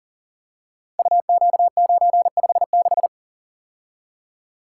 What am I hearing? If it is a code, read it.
Morse code, heard as a keyed tone, about 38 wpm, 700 Hz. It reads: UQ956
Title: Code Groups - 5 Letter:Numbers - 40wpm